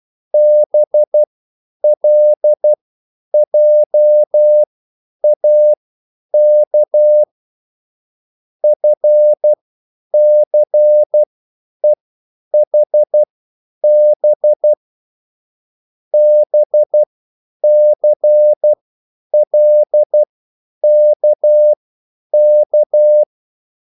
Morse code, heard as a keyed tone, about 12 wpm, spaced as for 9 wpm, 600 Hz. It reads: BLJAK FCEHB BCLKK